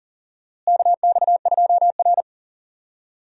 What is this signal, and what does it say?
Morse code, heard as a keyed tone, about 40 wpm, 700 Hz. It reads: KX2R
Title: USA Call Signs - 4 Characters - Repeated 3x - 40wpm